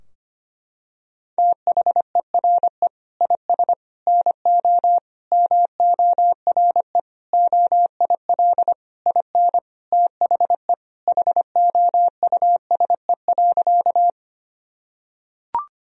Morse code, heard as a keyed tone, about 25 wpm, 700 Hz. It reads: THERE IS NO MORE OIL IN THE HOUSE.
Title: Sentences from Top 1000 Words - Review Part 4 - 25wpm